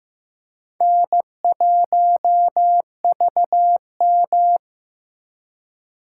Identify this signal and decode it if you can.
Morse code, heard as a keyed tone, about 15 wpm, 700 Hz. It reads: N1VM
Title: USA Call Signs - 4 Characters - 4x Character Spacing - 15wpm